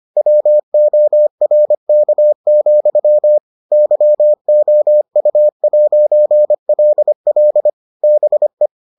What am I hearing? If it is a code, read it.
Morse code, heard as a keyed tone, about 25 wpm, 600 Hz. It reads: WORK, YOU'LL BE